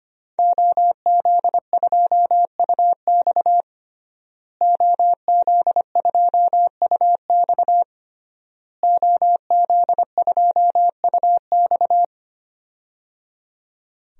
Morse code, heard as a keyed tone, about 25 wpm, 700 Hz. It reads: OZ2UX OZ2UX OZ2UX
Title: DXCC-100 Call Signs - 5 Characters - Repeated 3x - 25wpm